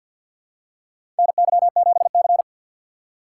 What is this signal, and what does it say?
Morse code, heard as a keyed tone, about 52 wpm, 700 Hz. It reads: NY7C
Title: USA Call Signs - 4 Characters - 2x Character Spacing - 50wpm